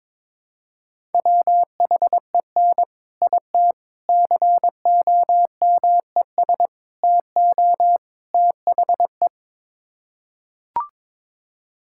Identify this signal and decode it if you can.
Morse code, heard as a keyed tone, about 23 wpm, 700 Hz. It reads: WHEN IT COMES TO THE
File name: practice / 22wpm / Build A Phrase - 5-Word Phrases - 22wpm